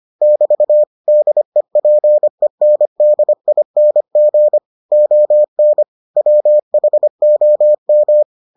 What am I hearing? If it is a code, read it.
Morse code, heard as a keyed tone, about 25 wpm, 600 Hz. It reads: <BT> DEPENDING ON WHOM